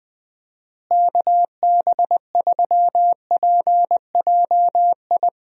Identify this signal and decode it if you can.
Morse code, heard as a keyed tone, about 20 wpm, 700 Hz. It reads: KB3PJI
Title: USA Call Signs - 6 Characters - Repeated 2x - 20wpm